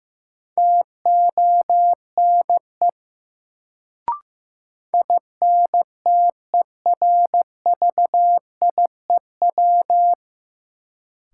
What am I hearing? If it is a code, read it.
Morse code, heard as a keyed tone, about 16 wpm, 700 Hz. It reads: TONE INTERVIEW
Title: Sentences from Top 1600 Words - New Word Drill - 15wpm